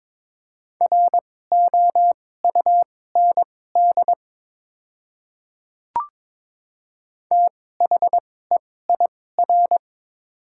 Morse code, heard as a keyed tone, about 22 wpm, spaced as for 16 wpm, 700 Hz. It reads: ROUND THEIR
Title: 2x Character Spacing - Top 300 Words - 22wpm